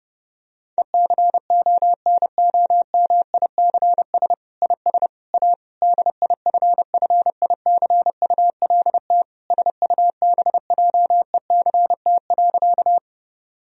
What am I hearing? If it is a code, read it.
Morse code, heard as a keyed tone, about 30 wpm, 700 Hz. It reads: ECONOMICS IS A DIFFICULT SUBJECT.
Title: Sentences from Top 1000 Words - Review Part 3 - 30wpm